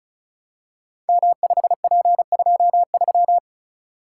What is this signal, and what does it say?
Morse code, heard as a keyed tone, about 36 wpm, 690 Hz. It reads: M5P23
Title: Code Groups - 5 Letter:Numbers - 35wpm